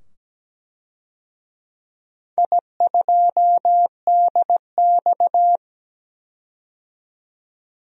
Morse code, heard as a keyed tone, about 17 wpm, 700 Hz. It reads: I2DX
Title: DXCC-100 Call Signs - 4 Characters - 17wpm